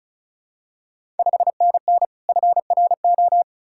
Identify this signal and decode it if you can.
Morse code, heard as a keyed tone, about 36 wpm, 700 Hz. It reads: 5NN FRO
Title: Ontario QSO Party Exchange 35wpm